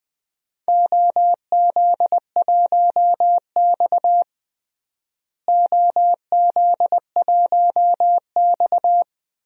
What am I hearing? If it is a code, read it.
Morse code, heard as a keyed tone, about 20 wpm, 700 Hz. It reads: OZ1X OZ1X
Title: DXCC-100 Call Signs - 4 Characters - Repeated 3x - 20wpm